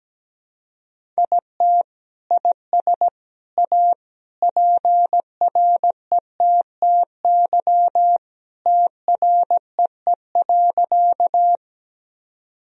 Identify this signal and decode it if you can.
Morse code, heard as a keyed tone, about 17 wpm, 700 Hz. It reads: IT IS A PRETTY TREE.
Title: Sentences from Top 1000 Words - Review Part 1 - 17wpm